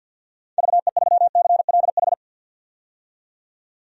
Morse code, heard as a keyed tone, about 51 wpm, 700 Hz. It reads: VE3CLH
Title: DXCC-100 Call Signs - 6 Characters - 50wpm